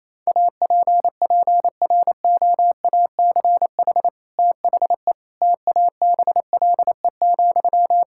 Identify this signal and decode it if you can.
Morse code, heard as a keyed tone, about 28 wpm, 700 Hz. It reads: APPROACH THE TABLE,
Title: Sentences from Top 1000 Words - Review Part 6 - 28wpm